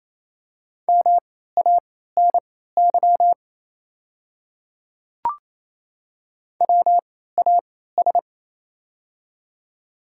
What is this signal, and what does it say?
Morse code, heard as a keyed tone, about 28 wpm, spaced as for 16 wpm, 700 Hz. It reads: MANY WAS
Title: 3x Character Spacing - Top 100 Words - 28wpm